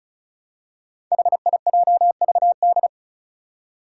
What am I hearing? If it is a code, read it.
Morse code, heard as a keyed tone, about 36 wpm, 700 Hz. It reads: HIJVD
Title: Five Challenging Letter-Number Combinations 35wpm